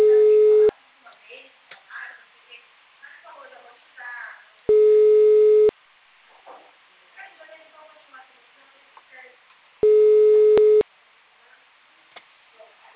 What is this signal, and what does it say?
Ringback tone (Europe, 425 Hz), with speech.
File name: out- -2010- - - .383